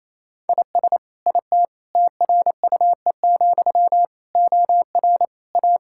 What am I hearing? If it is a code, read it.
Morse code, heard as a keyed tone, about 27 wpm, 700 Hz. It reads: IS IT TRUE, OR A
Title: Sentences from Top 1600 Words 28wpm